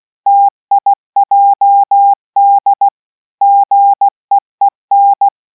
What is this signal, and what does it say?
Morse code, heard as a keyed tone, about 16 wpm, 800 Hz. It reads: TIJD GEEN